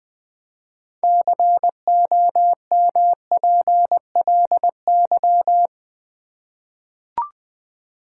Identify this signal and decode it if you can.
Morse code, heard as a keyed tone, about 20 wpm, 700 Hz. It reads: COMPLY
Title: 6 Letter Words - First 3 Letters Given - 20wpm